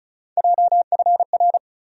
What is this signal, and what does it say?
Morse code, heard as a keyed tone, about 36 wpm, 700 Hz. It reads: JFR
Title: USA Call Sign Postfixes - 3 Letters 35wpm